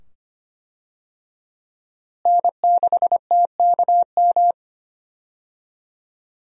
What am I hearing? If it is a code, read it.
Morse code, heard as a keyed tone, about 25 wpm, 700 Hz. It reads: N6TKM
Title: USA Call Signs - 5 Characters - Repeated 2x - 25wpm